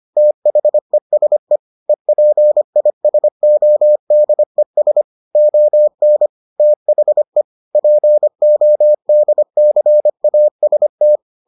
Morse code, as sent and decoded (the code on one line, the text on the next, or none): - .... ... / . .--. .. ... --- -.. ... / --- -. / - .... / .--. --- -.. -.-. .- ... -
THESE EPISODES ON THE PODCAST